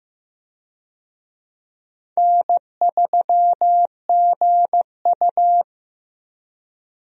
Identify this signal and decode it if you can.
Morse code, heard as a keyed tone, about 15 wpm, 700 Hz. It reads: N3GU
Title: USA Call Signs - 4 Characters - 15wpm